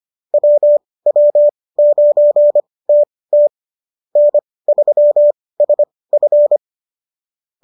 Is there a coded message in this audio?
Yes: Morse code from 0.3 to 6.6 s, heard as a keyed tone, about 25 wpm, spaced as for 18 wpm, 590 Hz. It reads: WW9TT N3SF